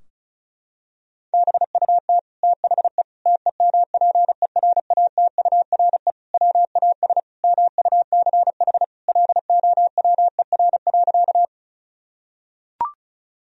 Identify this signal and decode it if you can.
Morse code, heard as a keyed tone, about 34 wpm, 700 Hz. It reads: BUT THE TEMPERATURE WAS MUCH LOWER.
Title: Sentences from Top 1000 Words - Review Part 4 - 35wpm